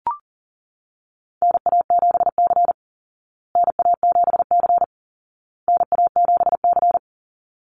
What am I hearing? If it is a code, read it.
Morse code, heard as a keyed tone, about 40 wpm, 700 Hz. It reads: NA7C NA7C NA7C
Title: Call Signs Repeated 2x - 40wpm